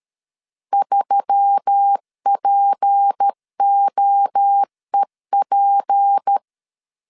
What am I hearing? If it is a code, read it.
Morse code, heard as a keyed tone, about 13 wpm, 800 Hz. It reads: 3POEP
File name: LEZIONI / 6 PKX0 / 6 P 12wpm 2033